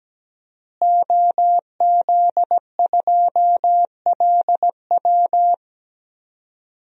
Morse code, heard as a keyed tone, about 17 wpm, 700 Hz. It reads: OZ2LW
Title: DXCC-100 Call Signs - 5 Characters - Repeated 3x - 17wpm